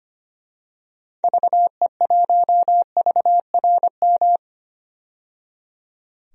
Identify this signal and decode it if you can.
Morse code, heard as a keyed tone, about 25 wpm, 700 Hz. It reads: VE1VRM
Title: DXCC-100 Call Signs - 6 Characters - 25wpm